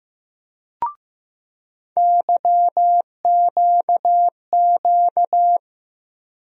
Morse code, heard as a keyed tone, about 15 wpm, 700 Hz. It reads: YQQ
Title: USA Call Sign Postfixes - 3 Letters 15wpm